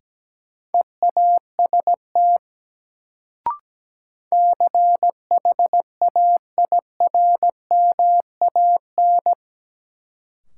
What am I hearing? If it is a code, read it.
Morse code, heard as a keyed tone, about 17 wpm, 700 Hz. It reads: EAST CHAIRMAN
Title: Sets of 3 Words - New Word Drill - 17wpm